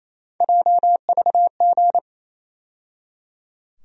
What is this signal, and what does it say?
Morse code, heard as a keyed tone, about 28 wpm, 700 Hz. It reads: JVG
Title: USA Call Sign Postfixes - 3 Letters 28wpm